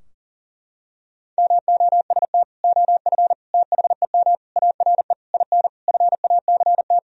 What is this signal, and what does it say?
Morse code, heard as a keyed tone, about 40 wpm, 700 Hz. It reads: MOST OF THEM ARE IN FACT
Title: Sentences from Top 1000 Words - Review Part 4 - 40wpm